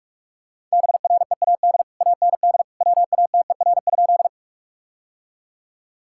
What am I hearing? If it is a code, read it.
Morse code, heard as a keyed tone, about 47 wpm, 700 Hz. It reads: BREAD AND WATER?
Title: Sentences from Top 1600 Words 45wpm